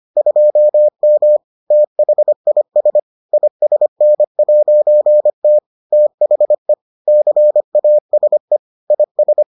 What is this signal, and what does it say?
Morse code, heard as a keyed tone, about 25 wpm, 600 Hz. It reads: 2M THIS ISN'T THE CASE IS